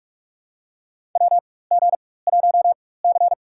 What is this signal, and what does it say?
Morse code, heard as a keyed tone, about 43 wpm, spaced as for 21 wpm, 700 Hz. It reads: WG1C
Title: USA Call Signs - 4 Characters - 4x Character Spacing - 45wpm